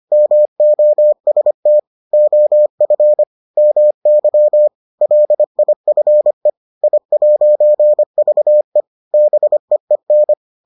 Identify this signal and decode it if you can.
Morse code, heard as a keyed tone, about 25 wpm, 600 Hz. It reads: MOST OF MY LIFE I'VE BEEN